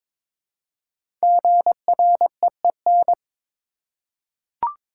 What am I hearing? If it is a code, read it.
Morse code, heard as a keyed tone, about 22 wpm, 700 Hz. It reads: GREEN
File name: 5 Letter Words - First 3 Letters Given - 22wpm